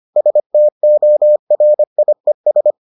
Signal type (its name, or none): Morse code